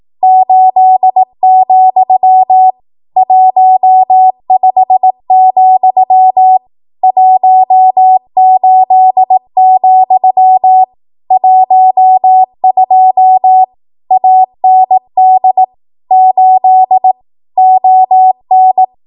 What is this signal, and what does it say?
Morse code, heard as a keyed tone, about 18 wpm, 750 Hz. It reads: 8, 15, 18, 12 AND 8 ON
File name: Archive / 18 WPM / 18WPM